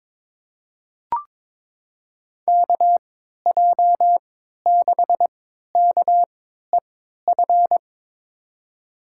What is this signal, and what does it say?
Morse code, heard as a keyed tone, about 22 wpm, spaced as for 12 wpm, 700 Hz. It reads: KJ6KEF
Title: USA Call Signs - 6 Characters - 3x Character Spacing - 22wpm